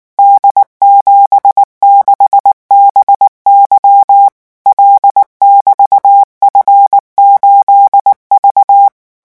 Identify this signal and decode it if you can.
Morse code, heard as a keyed tone, about 19 wpm, 800 Hz. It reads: D76BY L<BT>F8V
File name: morse mp3 / revision13-vitesse 18 mots